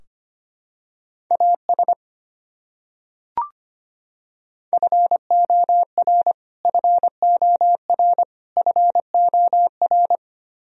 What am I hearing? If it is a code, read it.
Morse code, heard as a keyed tone, about 25 wpm, 700 Hz. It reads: AS FOR FOR FOR
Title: Top 100 Words Repeated 25wpm